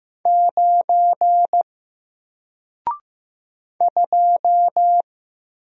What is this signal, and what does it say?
Morse code, heard as a keyed tone, about 15 wpm, 700 Hz. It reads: 9 2
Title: Single Letter-Number Alt 15wpm